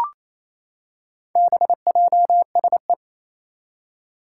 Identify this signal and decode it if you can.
Morse code, heard as a keyed tone, about 27 wpm, 700 Hz. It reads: BJSE
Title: Four Challenging Letter-Number Combinations 28wpm